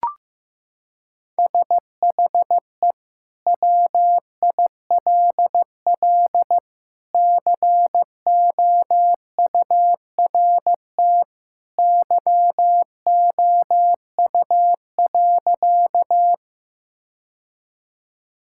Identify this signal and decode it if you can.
Morse code, heard as a keyed tone, about 15 wpm, 700 Hz. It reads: SHE WILL COURT YOU.
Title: Sentences from Top 1600 Words 15wpm